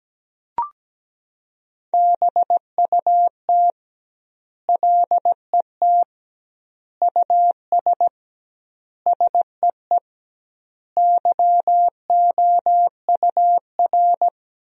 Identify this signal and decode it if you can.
Morse code, heard as a keyed tone, about 17 wpm, 700 Hz. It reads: BUT LET US SEE YOUR
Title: 2x Word-Spacing - Sentences from Top 500 Words - 17wpm